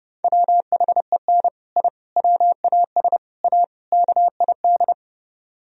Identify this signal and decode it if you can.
Morse code, heard as a keyed tone, about 28 wpm, 700 Hz. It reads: WHEN I WAS A KID